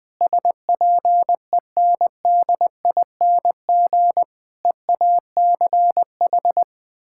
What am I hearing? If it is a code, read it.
Morse code, heard as a keyed tone, about 20 wpm, 700 Hz. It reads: SPENDING EACH
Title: Difficult Sentences with Hints - 20wpm